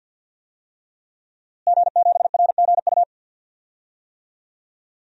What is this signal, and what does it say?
Morse code, heard as a keyed tone, about 51 wpm, 700 Hz. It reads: G7RGU
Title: DXCC-100 Call Signs - 5 Characters - Repeated 3x - 50wpm